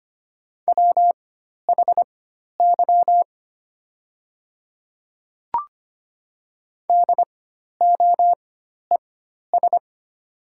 Morse code, heard as a keyed tone, about 25 wpm, spaced as for 12 wpm, 700 Hz. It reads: WHY DOES
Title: 4x Character Spacing - Top 500 Words - 25wpm